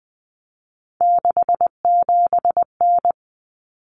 Morse code, heard as a keyed tone, about 20 wpm, 700 Hz. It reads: 67N
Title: Common Cut Numbers - 3-Digit 20wpm